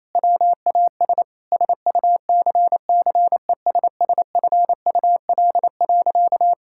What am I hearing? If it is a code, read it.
Morse code, heard as a keyed tone, about 28 wpm, 700 Hz. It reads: WAS SUCCESSFUL.
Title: Sentences from Top 1000 Words - Review Part 6 - 28wpm